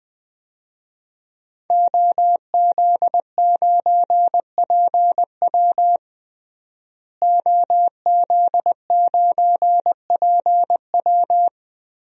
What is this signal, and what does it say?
Morse code, heard as a keyed tone, about 20 wpm, 700 Hz. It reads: OZ9PW OZ9PW
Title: DXCC-100 Call Signs - 5 Characters - Repeated 2x - 20wpm